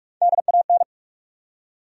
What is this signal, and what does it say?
Morse code, heard as a keyed tone, about 46 wpm, 700 Hz. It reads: DAN
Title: USA Common Names - Max 4 Letters - 45wpm